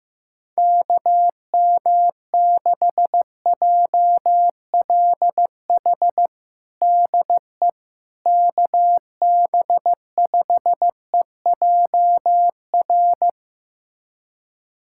Morse code, heard as a keyed tone, about 15 wpm, 700 Hz. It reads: KM6JLH DE KB5EJR
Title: USA Callsign DE Callsign - 6 Characters - 15wpm